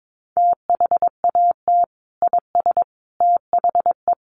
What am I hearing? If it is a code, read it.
Morse code, heard as a keyed tone, about 22 wpm, 700 Hz. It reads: THAT IS THE
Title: Sentences from Top 1300 Words 22wpm